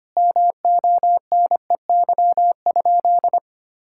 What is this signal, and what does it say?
Morse code, heard as a keyed tone, about 25 wpm, 700 Hz. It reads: MONEY?
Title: Sentences from Top 1600 Words 25wpm